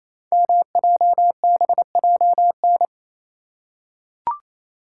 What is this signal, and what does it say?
Morse code, heard as a keyed tone, about 28 wpm, 700 Hz. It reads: MJBJN
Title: Code Groups - 5 Letters - 28wpm